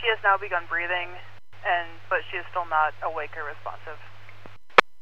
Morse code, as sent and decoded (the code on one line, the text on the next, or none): none